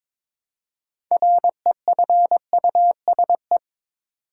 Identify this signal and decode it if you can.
Morse code, heard as a keyed tone, about 21 wpm, 700 Hz. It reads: REFUSE
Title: Six Letter Words - 22wpm